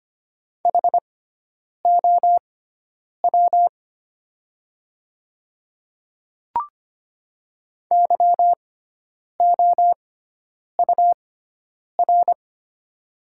Morse code, heard as a keyed tone, about 25 wpm, spaced as for 9 wpm, 700 Hz. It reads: HOW YOUR
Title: 6x Character Spacing - Top 100 Words - 25wpm